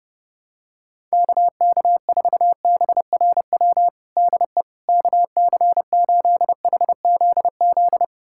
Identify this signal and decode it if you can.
Morse code, heard as a keyed tone, about 30 wpm, 700 Hz. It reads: KK4BRW DE KC8HZZ
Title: USA Callsign DE Callsign - 6 Characters - 30wpm